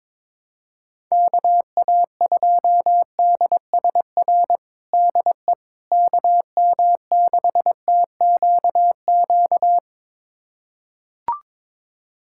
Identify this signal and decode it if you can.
Morse code, heard as a keyed tone, about 22 wpm, 700 Hz. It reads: KA2DSR DE KM6TQQ
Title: USA Callsign DE Callsign 22wpm